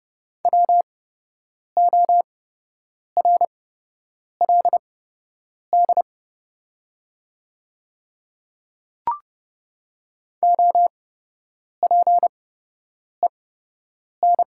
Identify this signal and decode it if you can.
Morse code, heard as a keyed tone, about 30 wpm, spaced as for 8 wpm, 700 Hz. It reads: WORLD OPEN
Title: 8x Character Spacing - Top 300 Words - 30wpm